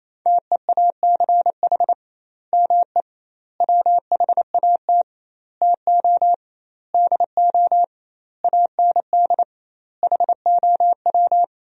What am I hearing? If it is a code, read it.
Morse code, heard as a keyed tone, about 28 wpm, 700 Hz. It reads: TEACH ME WHAT TO DO AND HOW
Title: 2x Word-Spacing - Sentences from Top 500 Words - 28wpm